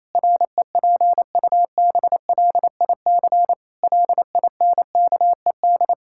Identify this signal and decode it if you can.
Morse code, heard as a keyed tone, about 28 wpm, 700 Hz. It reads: REPUBLIC LINKED
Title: Pangram Challenge v2 - 28wpm